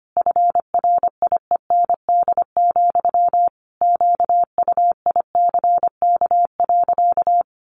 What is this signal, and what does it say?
Morse code, heard as a keyed tone, about 25 wpm, 700 Hz. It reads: FRIEND, QUICK.
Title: Sentences from Top 500 Words 25wpm